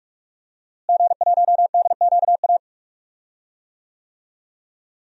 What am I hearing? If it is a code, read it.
Morse code, heard as a keyed tone, about 41 wpm, 700 Hz. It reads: G1DQA
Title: DXCC-100 Call Signs - 5 Characters - Repeated 2x - 45wpm